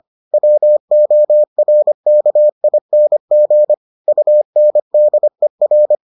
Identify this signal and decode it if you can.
Morse code, heard as a keyed tone, about 25 wpm, 600 Hz. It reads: WORKING UNDER